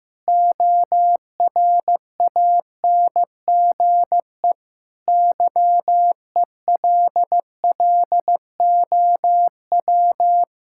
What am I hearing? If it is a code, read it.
Morse code, heard as a keyed tone, about 14 wpm, 700 Hz. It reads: ORANGE YELLOW
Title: Pangram Challenge v2 - 15wpm